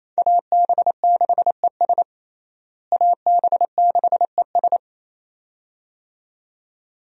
Morse code, heard as a keyed tone, about 27 wpm, 700 Hz. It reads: AB6ES AB6ES
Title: USA Call Signs - 5 Characters - Repeated 2x - 28wpm